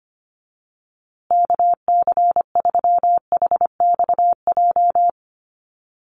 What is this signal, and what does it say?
Morse code, heard as a keyed tone, about 25 wpm, 700 Hz. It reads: KC3HXJ
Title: USA Call Signs Repeated 3x 25wpm